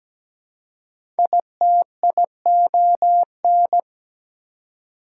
Morse code, heard as a keyed tone, about 17 wpm, 700 Hz. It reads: ITION
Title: Prefixes and Suffixes 17wpm